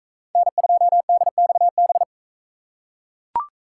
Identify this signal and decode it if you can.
Morse code, heard as a keyed tone, about 40 wpm, 700 Hz. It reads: N2DXB